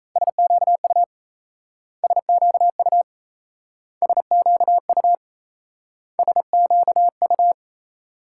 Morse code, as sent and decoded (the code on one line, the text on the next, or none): ... --.- ..- / ... --.- ..- / ... --.- ..- / ... --.- ..-
SQU SQU SQU SQU